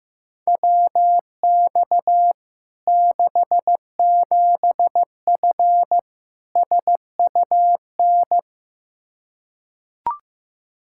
Morse code, heard as a keyed tone, about 16 wpm, 700 Hz. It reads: WX 67F SUN
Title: K1USN SST QSO Practice 15wpm